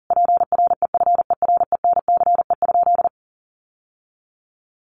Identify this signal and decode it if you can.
Morse code, heard as a keyed tone, about 40 wpm, 700 Hz. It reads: PREFERENCE?
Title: Sentences from Top 1500 Words 40wpm